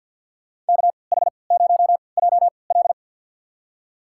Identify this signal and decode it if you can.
Morse code, heard as a keyed tone, about 53 wpm, spaced as for 29 wpm, 700 Hz. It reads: KH0JL
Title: USA Call Signs - 5 Characters - 3x Character Spacing - 50wpm